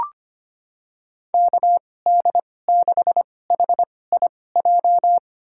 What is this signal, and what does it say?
Morse code, heard as a keyed tone, about 25 wpm, spaced as for 18 wpm, 700 Hz. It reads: KD6HIJ
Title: USA Call Signs - 6 Characters - 2x Character Spacing - 25wpm